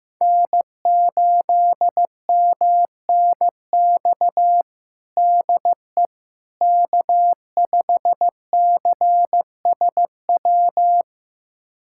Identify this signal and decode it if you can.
Morse code, heard as a keyed tone, about 15 wpm, 700 Hz. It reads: N8MNX DE K5CSW
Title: USA Callsign DE Callsign 15wpm